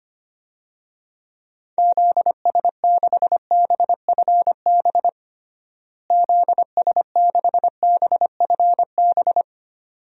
Morse code, heard as a keyed tone, about 25 wpm, 700 Hz. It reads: ZS6BFB ZS6BFB
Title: DXCC-100 Call Signs - 6 Characters - Repeated 2x - 25wpm